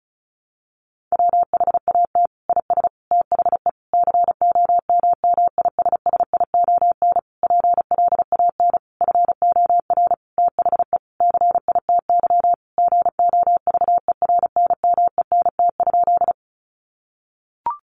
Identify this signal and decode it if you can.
Morse code, heard as a keyed tone, about 34 wpm, 700 Hz. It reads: WHAT IS THE COMMISSION PLAN FOR THE CITY GOVERNMENT?
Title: Sentences from Top 1400 Words 35wpm